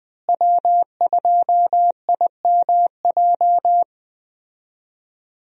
Morse code, heard as a keyed tone, about 20 wpm, 700 Hz. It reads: W2IMJ
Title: USA Call Signs - 5 Characters - Repeated 3x - 20wpm